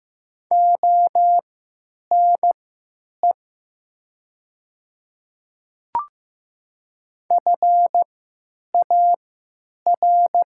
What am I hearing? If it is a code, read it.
Morse code, heard as a keyed tone, about 15 wpm, spaced as for 9 wpm, 700 Hz. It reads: ONE FAR
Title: 3x Character Spacing - Top 300 Words - 15wpm